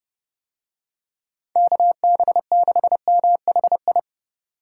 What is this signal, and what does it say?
Morse code, heard as a keyed tone, about 28 wpm, 700 Hz. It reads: KB6MHI